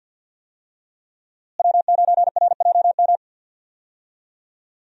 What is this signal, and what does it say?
Morse code, heard as a keyed tone, about 50 wpm, 700 Hz. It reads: W9RJM
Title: USA Call Signs - 5 Characters - Repeated 2x - 50wpm